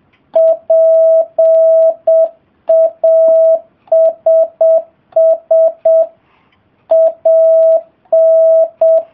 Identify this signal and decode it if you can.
Morse code, heard as a keyed tone, about 7 wpm, 660 Hz. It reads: PASS AN